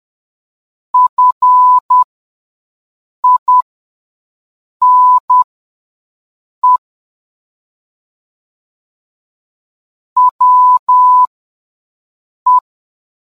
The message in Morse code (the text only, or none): FINE WE